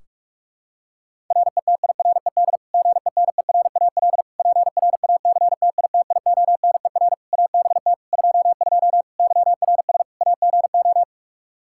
Morse code, heard as a keyed tone, about 48 wpm, 700 Hz. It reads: RETIRED GENERAL PRACTITIONER ABT 22 YRS AGO
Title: Practice QSO - Exchange 3 - More Info - 45wpm